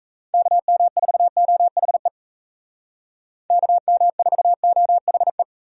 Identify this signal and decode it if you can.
Morse code, heard as a keyed tone, about 36 wpm, 700 Hz. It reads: KM4OHE KM4OHE